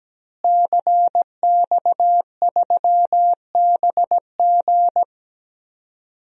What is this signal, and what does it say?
Morse code, heard as a keyed tone, about 17 wpm, 700 Hz. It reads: CX3BG
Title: DXCC-100 Call Signs - 5 Characters - Repeated 2x - 17wpm